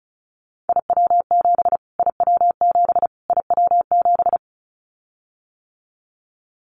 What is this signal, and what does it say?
Morse code, heard as a keyed tone, about 34 wpm, 700 Hz. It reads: IW7 IW7 IW7
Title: Callsign Prefixes for Top 25 DXCC - Repeated 3x - 35wpm